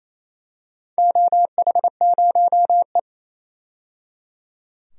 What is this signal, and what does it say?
Morse code, heard as a keyed tone, about 28 wpm, 700 Hz. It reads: OH0E
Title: DXCC-100 Call Signs - 4 Characters - 28wpm